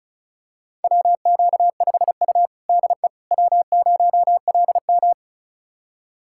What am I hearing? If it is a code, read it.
Morse code, heard as a keyed tone, about 36 wpm, 700 Hz. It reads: WQ5U DE W0LM
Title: USA Callsign DE Callsign - 4 Characters - 35wpm